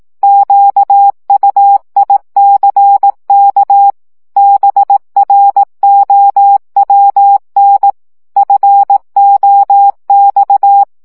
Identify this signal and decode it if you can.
Morse code, heard as a keyed tone, about 18 wpm, 800 Hz. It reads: QUICK BROWN FOX
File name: CW 15 baud